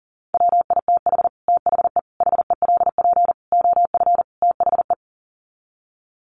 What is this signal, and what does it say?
Morse code, heard as a keyed tone, about 40 wpm, 700 Hz. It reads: WITH THE HELP OF THE
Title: Sets of 5 Words Encore 40wpm